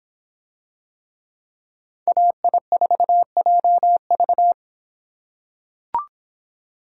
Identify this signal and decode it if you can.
Morse code, heard as a keyed tone, about 26 wpm, 700 Hz. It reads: AI4JV